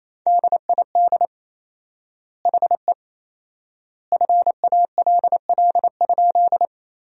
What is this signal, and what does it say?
Morse code, heard as a keyed tone, about 28 wpm, 700 Hz. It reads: DID HE FALL?
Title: 4x Word-Spacing - Sentences from Top 500 Words - 28wpm